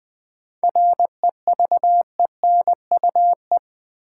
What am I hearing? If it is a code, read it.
Morse code, heard as a keyed tone, about 20 wpm, 700 Hz. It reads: REVENUE
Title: 7 Letter Words - First 3 Letters Given - 20wpm